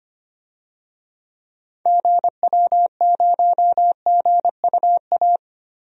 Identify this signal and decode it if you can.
Morse code, heard as a keyed tone, about 25 wpm, 700 Hz. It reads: GW0GUA
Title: DXCC-100 Call Signs - 6 Characters - Repeated 2x - 25wpm